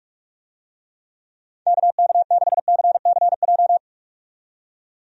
Morse code, heard as a keyed tone, about 47 wpm, 700 Hz. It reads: KK6CCJ
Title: USA Call Signs - 6 Characters - 45wpm